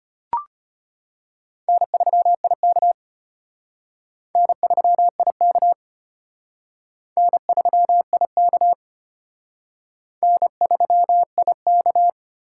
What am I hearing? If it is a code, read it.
Morse code, heard as a keyed tone, about 33 wpm, 700 Hz. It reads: N3IK N3IK N3IK N3IK